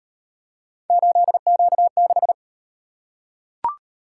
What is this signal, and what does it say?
Morse code, heard as a keyed tone, about 38 wpm, 700 Hz. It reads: 8Q6